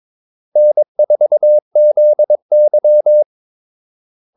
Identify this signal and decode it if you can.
Morse code, heard as a keyed tone, about 22 wpm, 600 Hz. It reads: N4ZY
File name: Callsigns-22wpm